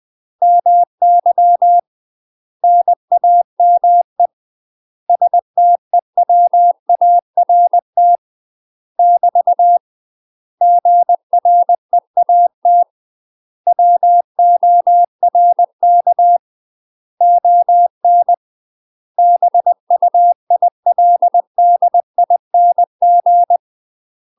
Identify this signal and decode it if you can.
Morse code, heard as a keyed tone, about 20 wpm, 700 Hz. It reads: MY NAME STEWART <BT> GREAT WORK ON BUILDING
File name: UK MASTER QSOs TRAINING / 20WPM / UK MASTER QSOs 0019